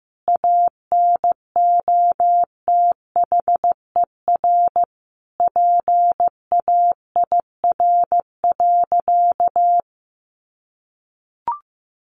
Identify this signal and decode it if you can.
Morse code, heard as a keyed tone, about 15 wpm, 700 Hz. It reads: ANOTHER PAIR.
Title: Sentences from Top 1200 Words 15wpm